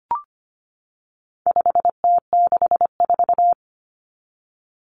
Morse code, heard as a keyed tone, about 25 wpm, 700 Hz. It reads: 5T64
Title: Common Cut Numbers - 4-Digit 25wpm